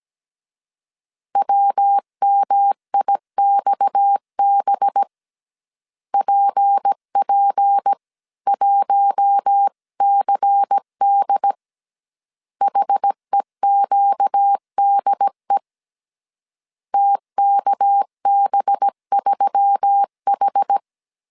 Morse code, heard as a keyed tone, about 17 wpm, 800 Hz. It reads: WMIXB PP1CD HEQDE TKB3H
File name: CW / DET / 1 6 1500C S A 16wpm 2635